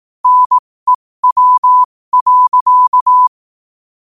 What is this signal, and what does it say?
Morse code, heard as a keyed tone, about 18 wpm, spaced as for 15 wpm, 1000 Hz. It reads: NEW.
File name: qotdfiles / 15wpm